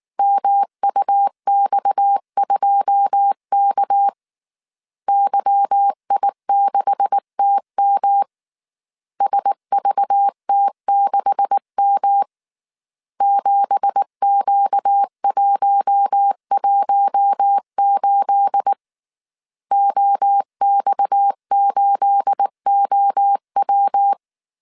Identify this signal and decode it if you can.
Morse code, heard as a keyed tone, about 19 wpm, 800 Hz. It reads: MUX2K YI6TM SVT6M 7Q118 OX8OW
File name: CW / DET / 1 9 1500C S C 18wpm 2539